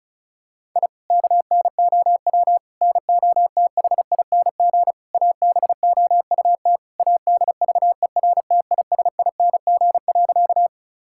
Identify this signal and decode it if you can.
Morse code, heard as a keyed tone, about 36 wpm, 700 Hz. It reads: I KNOW NOTHING ABOUT ADVERTISING.
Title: Sentences from Top 1000 Words - Review Part 6 - 35wpm